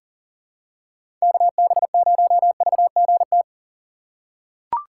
Morse code, heard as a keyed tone, about 37 wpm, 700 Hz. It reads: KB0VGT
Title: USA Call Signs - 6 Characters - 4x Character Spacing - 40wpm